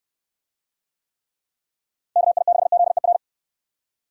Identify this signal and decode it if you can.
Morse code, heard as a keyed tone, about 69 wpm, 700 Hz. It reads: KI6ZER